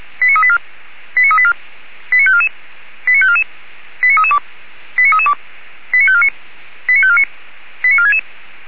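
Five-tone selective call (ZVEI), heard as quick runs of five tones, tones of about 70 ms. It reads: 89384 89384 89544 89544 89332 89332 89549 89549 89466
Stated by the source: Five-tone ZVEI selective call received by SDR.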